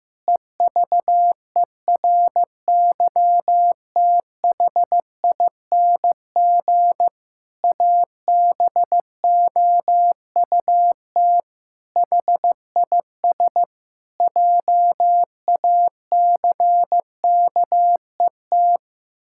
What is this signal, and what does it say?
Morse code, heard as a keyed tone, about 15 wpm, 700 Hz. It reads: EVERYTHING ABOUT HIS JACKET
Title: Pangram Challenge v2 - 15wpm